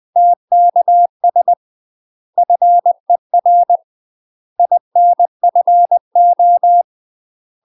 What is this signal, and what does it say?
Morse code, heard as a keyed tone, about 20 wpm, 700 Hz. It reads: TKS FER INFO
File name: UK MASTER QSOs TRAINING / 20WPM / UK MASTER QSOs 0061